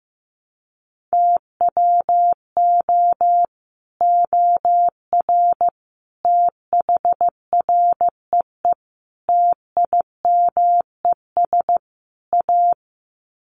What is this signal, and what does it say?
Morse code, heard as a keyed tone, about 15 wpm, 700 Hz. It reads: TWO OR THREE TIMES A
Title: Sets of 5 Words Encore2 15wpm